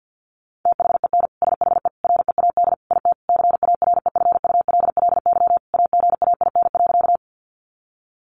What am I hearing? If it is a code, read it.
Morse code, heard as a keyed tone, about 48 wpm, 700 Hz. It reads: THEN SHE READ IT CAREFULLY AGAIN.
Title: Sentences from Top 1300 Words 50wpm